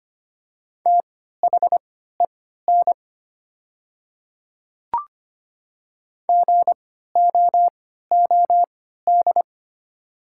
Morse code, heard as a keyed tone, about 25 wpm, spaced as for 14 wpm, 700 Hz. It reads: THEN GOOD
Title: 3x Character Spacing - Top 200 Words - 25wpm